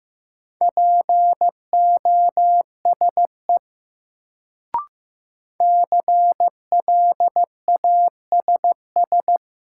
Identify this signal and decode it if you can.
Morse code, heard as a keyed tone, about 15 wpm, 700 Hz. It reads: POSE CLASS
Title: Sentences from Top 1000 Words - Review Part 2 - New Word Drill - 15wpm